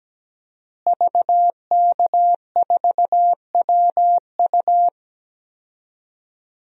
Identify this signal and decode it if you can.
Morse code, heard as a keyed tone, about 17 wpm, 700 Hz. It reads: VK4WU
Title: DXCC-100 Call Signs - 5 Characters - Repeated 3x - 17wpm